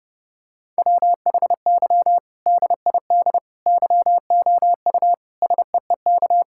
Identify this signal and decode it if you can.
Morse code, heard as a keyed tone, about 30 wpm, 700 Hz. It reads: WHY DID YOU SEEK
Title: Sentences from Top 1600 Words 30wpm